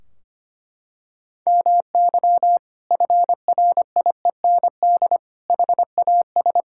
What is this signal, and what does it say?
Morse code, heard as a keyed tone, about 25 wpm, 700 Hz. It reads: MY FRIEND HAS
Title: Sentences from Top 1600 Words 25wpm